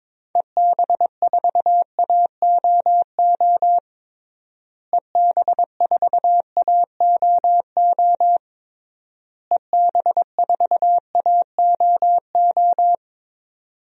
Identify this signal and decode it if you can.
Morse code, heard as a keyed tone, about 22 wpm, 700 Hz. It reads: EB4AOO EB4AOO EB4AOO
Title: DXCC-100 Call Signs - 6 Characters - Repeated 3x - 22wpm